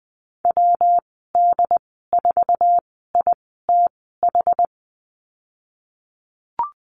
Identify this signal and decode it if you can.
Morse code, heard as a keyed tone, about 20 wpm, spaced as for 15 wpm, 700 Hz. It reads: WD4ITH
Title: 122 Lesson 20g - Number 3 - Call Signs 20wpm